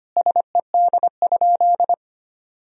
Morse code, heard as a keyed tone, about 25 wpm, 700 Hz. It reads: SED?
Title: Sentences from Top 100 Words and Abbreviations 25wpm